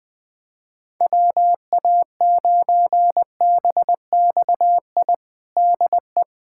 Morse code, heard as a keyed tone, about 20 wpm, 700 Hz. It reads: WA9BXI DE
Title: K1USN SST QSO Practice 20wpm